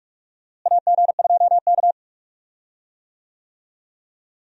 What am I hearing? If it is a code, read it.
Morse code, heard as a keyed tone, about 46 wpm, 700 Hz. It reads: AG2K
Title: USA Call Signs - 4 Characters - Repeated 3x - 45wpm